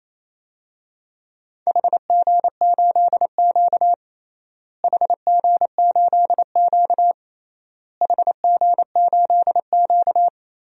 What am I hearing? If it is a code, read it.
Morse code, heard as a keyed tone, about 28 wpm, 700 Hz. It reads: HG8Q HG8Q HG8Q
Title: DXCC-100 Call Signs - 4 Characters - Repeated 3x - 28wpm